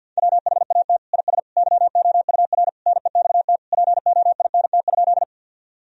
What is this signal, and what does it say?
Morse code, heard as a keyed tone, about 48 wpm, 700 Hz. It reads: WHAT IS YOUR NEXT POINT?
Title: Sentences from Top 1000 Words - Review Part 1 - 50wpm